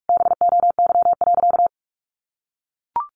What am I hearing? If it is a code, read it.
Morse code, heard as a keyed tone, about 44 wpm, 700 Hz. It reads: BOY.
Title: Sentences from Top 1400 Words 45wpm